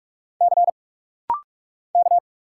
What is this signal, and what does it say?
Morse code, heard as a keyed tone, about 44 wpm, 700 Hz. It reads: C K
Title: Single Letter-Number - Rapid-Fire 45wpm